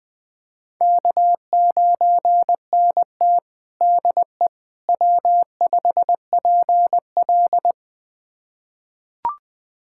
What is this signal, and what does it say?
Morse code, heard as a keyed tone, about 21 wpm, 700 Hz. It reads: K9NT DE W5PL
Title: USA Callsign DE Callsign - 4 Characters - 20wpm